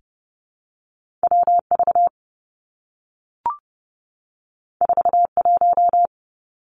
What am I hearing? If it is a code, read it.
Morse code, heard as a keyed tone, about 30 wpm, 700 Hz. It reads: W V 4 1
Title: Two Challenging Letter-Number Combinations 30wpm